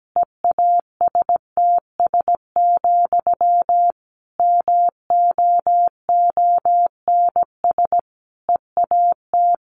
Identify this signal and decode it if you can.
Morse code, heard as a keyed tone, about 17 wpm, 700 Hz. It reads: EASTS, MOONS EAT
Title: A Sound Of Thunder 17wpm